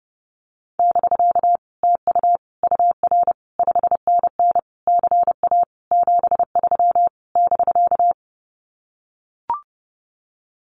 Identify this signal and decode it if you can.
Morse code, heard as a keyed tone, about 30 wpm, 700 Hz. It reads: <BK> TU UR 5NN CA 73 <BK>
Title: Common QSO Phrases Encore 30wpm